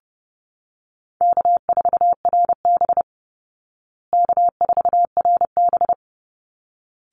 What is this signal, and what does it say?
Morse code, heard as a keyed tone, about 30 wpm, 700 Hz. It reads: K4RB K4RB
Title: Easy Call Signs Repeated 2x - 30wpm